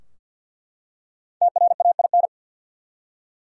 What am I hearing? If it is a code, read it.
Morse code, heard as a keyed tone, about 50 wpm, 700 Hz. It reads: TRAIN
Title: Five Letter Words - 50wpm